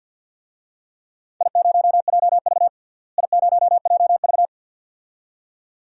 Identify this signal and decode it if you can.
Morse code, heard as a keyed tone, about 48 wpm, 700 Hz. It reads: I0JV I0JV
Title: DXCC-100 Call Signs - 4 Characters - Repeated 2x - 50wpm